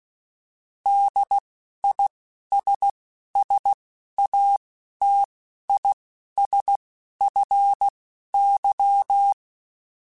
Morse code, heard as a keyed tone, about 16 wpm, spaced as for 11 wpm, 800 Hz. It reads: DISSATISFY
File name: General / General-mp3s / ISS @10